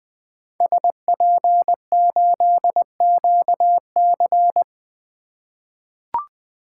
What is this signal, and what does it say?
Morse code, heard as a keyed tone, about 20 wpm, 700 Hz. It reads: SP8QC
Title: 2023 POTA Hunter Call Signs - 20wpm